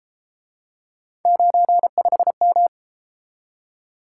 Morse code, heard as a keyed tone, about 30 wpm, 700 Hz. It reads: 95M